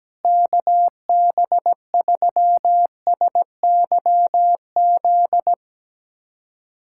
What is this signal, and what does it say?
Morse code, heard as a keyed tone, about 17 wpm, 700 Hz. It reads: KB3SYZ
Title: USA Callsign DE Callsign 17wpm